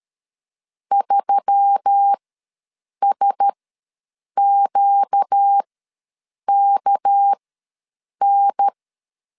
Morse code, heard as a keyed tone, about 13 wpm, spaced as for 7 wpm, 800 Hz. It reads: 3SQKN
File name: LEZIONI / 6 PKX0 / 6 PK ws 12wpm 3116